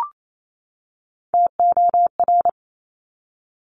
Morse code, heard as a keyed tone, about 28 wpm, 700 Hz. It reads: TOR
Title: Top Three Letter Combinations in Words 28wpm